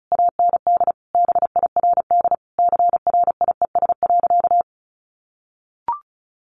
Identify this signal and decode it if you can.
Morse code, heard as a keyed tone, about 34 wpm, 700 Hz. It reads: AND BIRD CRIES.
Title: A Sound Of Thunder 35wpm